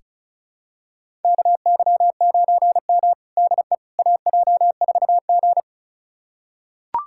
Morse code, heard as a keyed tone, about 34 wpm, 700 Hz. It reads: KY9M DE AJ4G
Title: USA Callsign DE Callsign - 4 Characters - 35wpm